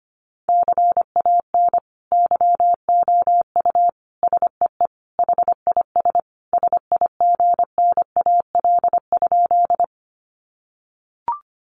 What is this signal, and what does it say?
Morse code, heard as a keyed tone, about 25 wpm, 700 Hz. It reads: CAN YOU SEE HIS SIGNAL?
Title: Sentences from Top 1400 Words 25wpm